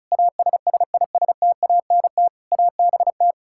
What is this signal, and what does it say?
Morse code, heard as a keyed tone, about 36 wpm, 700 Hz. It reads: ASSISTANT ABT
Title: Practice QSO - Exchange 3 - More Info - 35wpm